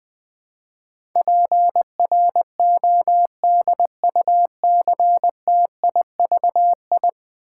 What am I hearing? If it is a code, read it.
Morse code, heard as a keyed tone, about 20 wpm, 700 Hz. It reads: PRODUCTIVI
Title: Build A Word - 5 Letters At A Time - 12-Letter Words - 20wpm